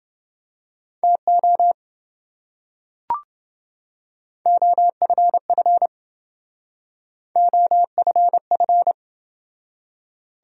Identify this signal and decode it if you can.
Morse code, heard as a keyed tone, about 28 wpm, 700 Hz. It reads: TO OFF OFF